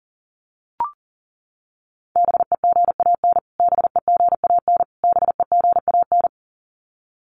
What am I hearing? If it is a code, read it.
Morse code, heard as a keyed tone, about 37 wpm, 700 Hz. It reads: BEGAN BEGAN BEGAN
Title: Top 300 Words Repeated 40wpm